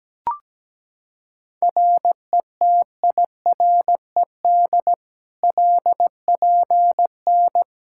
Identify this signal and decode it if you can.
Morse code, heard as a keyed tone, about 17 wpm, 700 Hz. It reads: RETIRED LPN
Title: Practice QSO - Exchange 3 - More Info - 17wpm